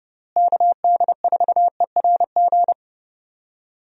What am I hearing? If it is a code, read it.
Morse code, heard as a keyed tone, about 31 wpm, 700 Hz. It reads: KD4ERG
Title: USA Call Signs - 6 Characters - Repeated 2x - 30wpm